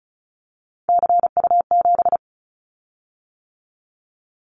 Morse code, heard as a keyed tone, about 36 wpm, 690 Hz. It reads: CU7
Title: Callsign Prefixes for Top 50 DXCC - 35wpm